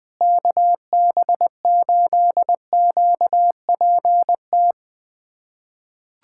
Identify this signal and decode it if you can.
Morse code, heard as a keyed tone, about 21 wpm, 700 Hz. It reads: KB8QPT
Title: USA Callsign DE Callsign - 6 Characters - 20wpm